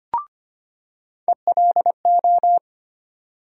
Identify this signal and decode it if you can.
Morse code, heard as a keyed tone, about 25 wpm, 700 Hz. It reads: ELO
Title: USA Call Sign Postfixes - 3 Letters 25wpm